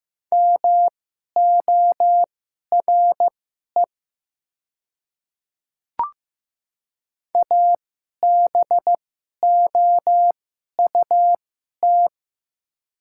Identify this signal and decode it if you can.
Morse code, heard as a keyed tone, about 15 wpm, spaced as for 11 wpm, 700 Hz. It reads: MORE ABOUT
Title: 2x Character Spacing - Top 100 Words - 15wpm